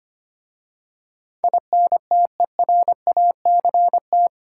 Morse code, heard as a keyed tone, about 25 wpm, 700 Hz. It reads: INTERACT
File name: 8 Letter Words - First 3 Letters Given - 25wpm